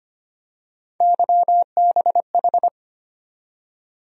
Morse code, heard as a keyed tone, about 25 wpm, 700 Hz. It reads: YBH
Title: USA Call Sign Postfixes - 3 Letters 25wpm